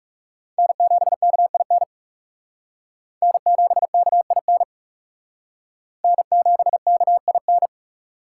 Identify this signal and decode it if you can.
Morse code, heard as a keyed tone, about 38 wpm, 700 Hz. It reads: N7KIN N7KIN N7KIN